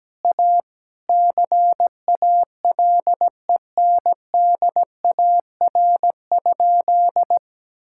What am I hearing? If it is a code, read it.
Morse code, heard as a keyed tone, about 17 wpm, 700 Hz. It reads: A CALENDAR?
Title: Sentences from Top 1600 Words 17wpm